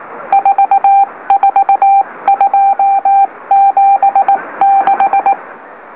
Morse code, heard as a keyed tone, about 19 wpm, 790 Hz. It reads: 44276
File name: M04- -2345utc-LOLO